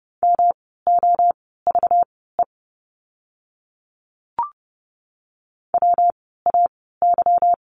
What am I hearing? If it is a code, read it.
Morse code, heard as a keyed tone, about 30 wpm, spaced as for 17 wpm, 700 Hz. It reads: MOVE WAY
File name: course / 30:10wpm / 237 Lesson 38f - Letter q - Words 30wpm